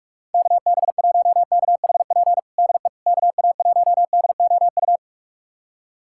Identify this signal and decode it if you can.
Morse code, heard as a keyed tone, about 46 wpm, 700 Hz. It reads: KB1KHP DE KA1DOU
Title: USA Callsign DE Callsign 45wpm